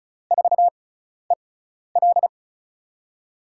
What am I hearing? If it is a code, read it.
Morse code, heard as a keyed tone, about 36 wpm, spaced as for 12 wpm, 700 Hz. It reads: 4EL
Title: USA Call Signs - 5 Characters - 6x Character Spacing - 35wpm